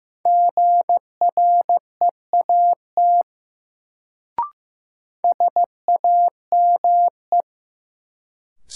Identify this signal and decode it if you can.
Morse code, heard as a keyed tone, about 15 wpm, 700 Hz. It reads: GREAT SAME
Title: Sets of 2 Words - New Word Drill - 15wpm